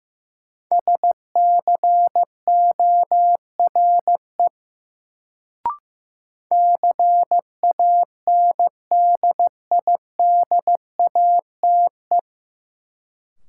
Morse code, heard as a keyed tone, about 15 wpm, 700 Hz. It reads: SCORE CANDIDATE
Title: Sentences from Top 1600 Words - New Word Drill Alt - 15wpm